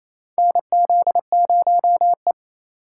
Morse code, heard as a keyed tone, about 28 wpm, 700 Hz. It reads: NZ0E
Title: USA Call Signs - 4 Characters - 4x Character Spacing - 28wpm